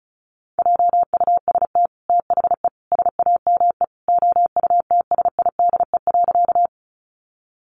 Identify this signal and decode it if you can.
Morse code, heard as a keyed tone, about 34 wpm, 700 Hz. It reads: JUST THE SAME OUTSIDE.
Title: Sentences from Top 900 Words 35wpm